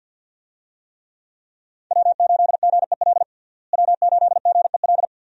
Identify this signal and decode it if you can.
Morse code, heard as a keyed tone, about 52 wpm, 700 Hz. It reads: W8GEL W8GEL
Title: USA Call Signs - 5 Characters - Repeated 2x - 50wpm